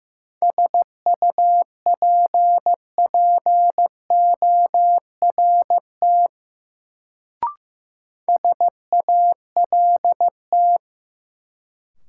Morse code, heard as a keyed tone, about 15 wpm, 700 Hz. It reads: SUPPORT SALT
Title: Sentences from Top 800 Words - New Word Drill - 15wpm